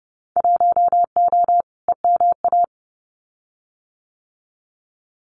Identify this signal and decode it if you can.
Morse code, heard as a keyed tone, about 30 wpm, 700 Hz. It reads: 1O EMA
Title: Winter Field Day Exchanges Encore 30wpm